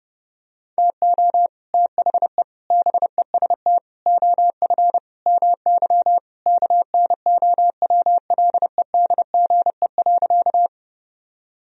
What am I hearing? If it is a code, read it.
Morse code, heard as a keyed tone, about 30 wpm, 700 Hz. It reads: TO THE BEST OF MY KNOWLEDGE.
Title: Sentences from Top 1000 Words - Review Part 3 - 30wpm